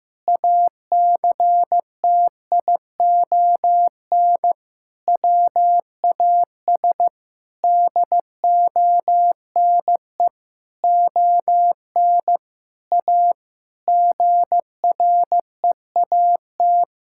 Morse code, heard as a keyed tone, about 16 wpm, 700 Hz. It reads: ACTION WAS DONE ON A GREAT
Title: Sentences from Top 1000 Words - Review Part 5 - 15wpm